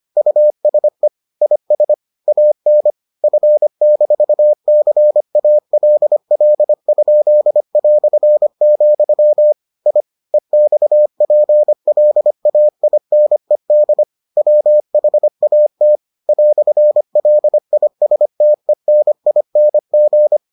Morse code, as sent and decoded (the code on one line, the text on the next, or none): ..- ... / .. ... / .- -. / ..-. -....- -.-. .- .-.. .-.. ..--.. .-..-. --..-- / .. / . -..- .--. .-.. .- .. -. -.. / .-- .... .- - / .-..-. .-.. .. ... - . -. .. -. --.
USE IS AN F-CALL?", I EXPLAINED WHAT "LISTENING